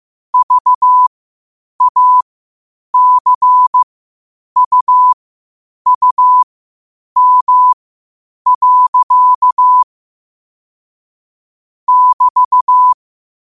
Morse code, heard as a keyed tone, about 15 wpm, spaced as for 8 wpm, 1000 Hz. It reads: VACUUM. <BT>
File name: qotdfiles / 8wpm